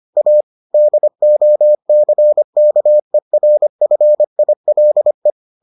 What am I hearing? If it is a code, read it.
Morse code, heard as a keyed tone, about 25 wpm, 600 Hz. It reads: A DOCKERFILE